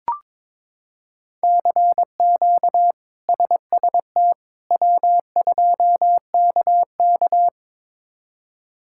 Morse code, heard as a keyed tone, about 22 wpm, 700 Hz. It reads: CQ SST W2KK
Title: K1USN SST QSO Practice - 22wpm